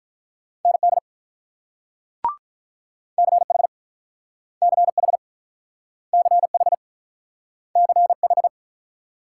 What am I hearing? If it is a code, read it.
Morse code, heard as a keyed tone, about 48 wpm, 700 Hz. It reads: ND CH CH CH CH